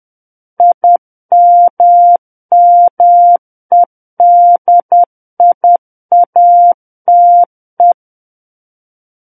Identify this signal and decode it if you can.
Morse code, heard as a keyed tone, about 10 wpm, 700 Hz. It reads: IMMEDIATE
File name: Book One / 01-Chapter 1 - 10 WPM